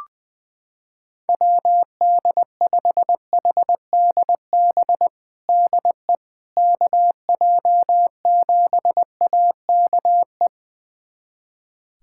Morse code, heard as a keyed tone, about 20 wpm, 700 Hz. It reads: WD5HDB DE KJ7AKE
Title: USA Callsign DE Callsign - 6 Characters - 20wpm